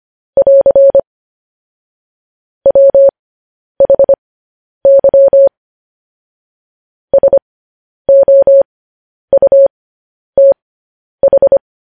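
Morse code, heard as a keyed tone, about 25 wpm, spaced as for 10 wpm, 550 Hz. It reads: <AR> WHY SOUTH
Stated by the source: Morse code news headlines sent as a reading transcription, preceded by vvv.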